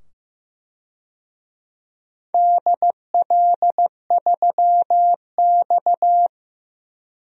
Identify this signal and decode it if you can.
Morse code, heard as a keyed tone, about 15 wpm, 700 Hz. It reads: DL3X
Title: DXCC-100 Call Signs - 4 Characters - 15wpm